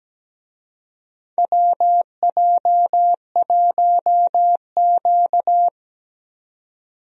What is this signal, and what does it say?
Morse code, heard as a keyed tone, about 17 wpm, 700 Hz. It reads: WJ1Q
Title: USA Call Signs - 4 Characters - 17wpm